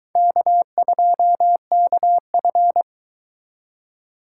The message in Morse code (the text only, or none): K2KF